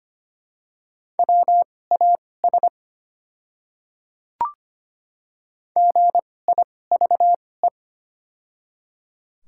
Morse code, heard as a keyed tone, about 25 wpm, spaced as for 18 wpm, 700 Hz. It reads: WAS GIVE
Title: 2x Character Spacing - Top 200 Words - 25wpm